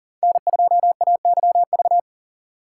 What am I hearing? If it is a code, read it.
Morse code, heard as a keyed tone, about 40 wpm, 700 Hz. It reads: N2AYV
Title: USA Call Signs - 5 Characters - 2x Character Spacing - 40wpm